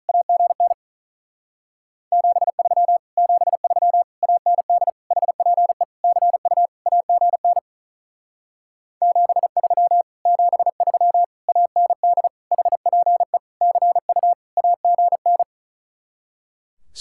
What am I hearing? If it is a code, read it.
Morse code, heard as a keyed tone, about 43 wpm, 700 Hz. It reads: AGN 73 73 AND HPE CU AGN 73 73 AND HPE CU AGN
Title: Speed-Racing Common QSO Phrases Encore - 35wpm